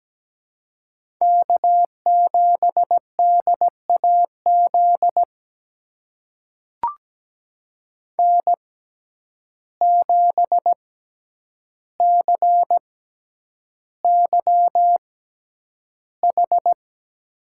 Morse code, heard as a keyed tone, about 17 wpm, spaced as for 9 wpm, 700 Hz. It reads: K7DAZ N7CYH
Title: USA Call Signs - 5 Characters - 6x Character Spacing - 17wpm